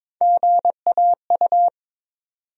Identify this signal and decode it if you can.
Morse code, heard as a keyed tone, about 22 wpm, 700 Hz. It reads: GAU
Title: USA Call Sign Postfixes - 3 Letters 22wpm